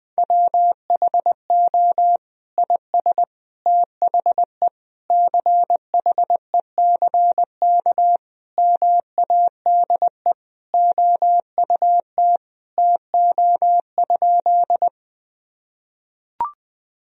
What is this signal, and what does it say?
Morse code, heard as a keyed tone, about 20 wpm, 700 Hz. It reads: WHO IS THE CHECK MADE OUT TO?
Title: Sentences from Top 1000 Words - Review Part 3 - 20wpm